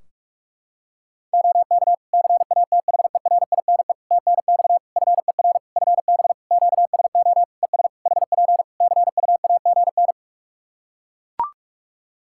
Morse code, heard as a keyed tone, about 41 wpm, 700 Hz. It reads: OK CATHERINE TNX FER FB QSO ES HP CUAGN
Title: Practice QSO - Exchange 4 - Ending - 45wpm